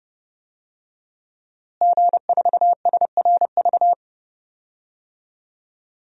Morse code, heard as a keyed tone, about 29 wpm, 700 Hz. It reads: G4SRV
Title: DXCC-100 Call Signs - 5 Characters - Repeated 3x - 30wpm